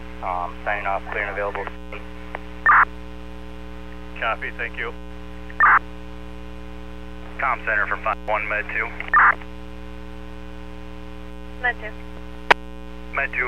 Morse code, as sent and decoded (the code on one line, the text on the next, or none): none